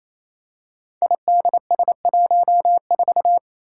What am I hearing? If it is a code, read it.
Morse code, heard as a keyed tone, about 27 wpm, 700 Hz. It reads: IDS14
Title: Five Challenging Letter-Number Combinations 28wpm